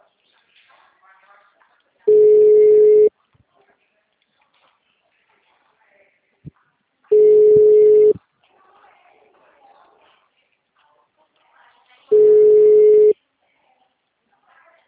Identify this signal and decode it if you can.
Ringback tone (Europe, 425 Hz), with speech.